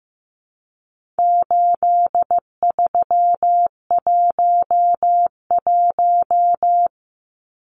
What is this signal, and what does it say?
Morse code, heard as a keyed tone, about 15 wpm, 700 Hz. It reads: 8311
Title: Numbers - 4-Digit 15wpm